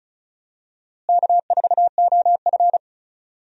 Morse code, heard as a keyed tone, about 34 wpm, 700 Hz. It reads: K4OF
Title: USA Call Signs - 4 Characters - Repeated 2x - 35wpm